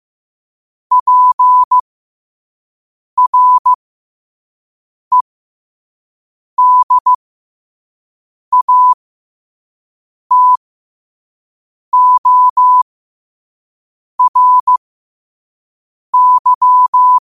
Morse code, heard as a keyed tone, about 15 wpm, spaced as for 5 wpm, 1000 Hz. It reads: PREDATORY
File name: qotdfiles / 5wpm